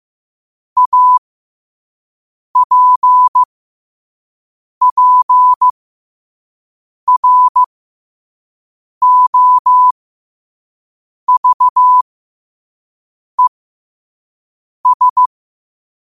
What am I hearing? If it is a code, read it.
Morse code, heard as a keyed tone, about 15 wpm, spaced as for 5 wpm, 1000 Hz. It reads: APPROVES
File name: qotdfiles / 5wpm